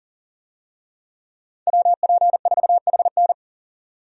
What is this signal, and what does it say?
Morse code, heard as a keyed tone, about 40 wpm, 700 Hz. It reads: WP4HN
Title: USA Call Signs - 5 Characters - 40wpm